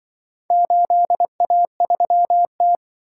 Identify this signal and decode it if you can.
Morse code, heard as a keyed tone, about 25 wpm, 700 Hz. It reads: 8A3T
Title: Speed-Racing - Four Letter-Number Combinations - 20wpm